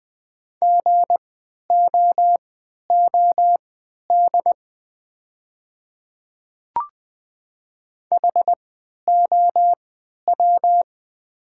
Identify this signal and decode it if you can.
Morse code, heard as a keyed tone, about 20 wpm, spaced as for 11 wpm, 700 Hz. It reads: GOOD HOW
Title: 3x Character Spacing - Top 500 Words - 20wpm